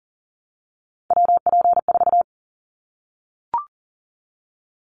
Morse code, heard as a keyed tone, about 40 wpm, 700 Hz. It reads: WP4
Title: Callsign Prefixes for Top 25 DXCC - 40wpm